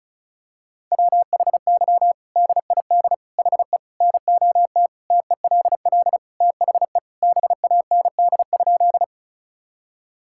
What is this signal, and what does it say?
Morse code, heard as a keyed tone, about 34 wpm, 700 Hz. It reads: WHY DID HE NOT TELL THE BAND?
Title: Sentences from Top 1600 Words 35wpm